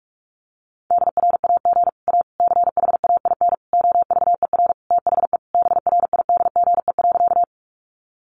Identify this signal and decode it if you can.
Morse code, heard as a keyed tone, about 44 wpm, 700 Hz. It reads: DRAG A CHAIN OVER THE BRIDGE.
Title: Sentences from Top 1500 Words 45wpm